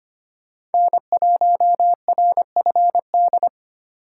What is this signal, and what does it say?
Morse code, heard as a keyed tone, about 32 wpm, 700 Hz. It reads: N1RFD